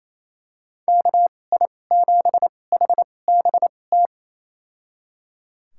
Morse code, heard as a keyed tone, about 28 wpm, spaced as for 20 wpm, 700 Hz. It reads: KI7HBT
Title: USA Call Signs - 6 Characters - 2x Character Spacing - 28wpm